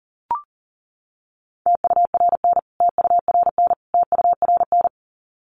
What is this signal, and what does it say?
Morse code, heard as a keyed tone, about 40 wpm, 700 Hz. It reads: TURN TURN TURN
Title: Top 300 Words Repeated 40wpm